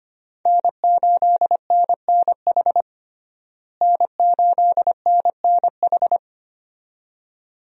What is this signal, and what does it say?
Morse code, heard as a keyed tone, about 25 wpm, 700 Hz. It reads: N8NNH N8NNH
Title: USA Call Signs - 5 Characters - Repeated 2x - 25wpm